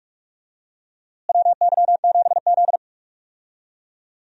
Morse code, heard as a keyed tone, about 46 wpm, 700 Hz. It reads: WY7Z
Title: USA Call Signs - 4 Characters - Repeated 2x - 45wpm